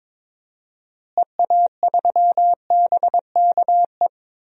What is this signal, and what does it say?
Morse code, heard as a keyed tone, about 22 wpm, 700 Hz. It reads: EA3BKE
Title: DXCC-100 Call Signs - 6 Characters - Repeated 3x - 22wpm